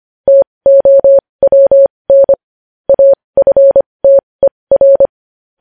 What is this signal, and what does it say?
Morse code, heard as a keyed tone, about 25 wpm, spaced as for 20 wpm, 550 Hz. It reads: TOWN AFTER
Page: News Headlines in Morse Code at 20 WPM